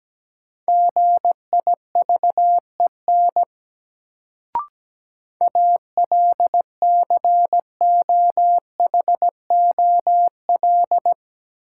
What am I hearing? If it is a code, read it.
Morse code, heard as a keyed tone, about 17 wpm, 700 Hz. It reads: GIVEN ALCOHOL
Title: Sentences from Top 1400 Words - New Word Drill - 17wpm